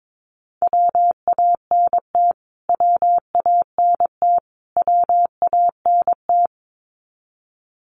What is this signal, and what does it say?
Morse code, heard as a keyed tone, about 22 wpm, 700 Hz. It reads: WANT WANT WANT
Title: Top 300 Words Repeated 22wpm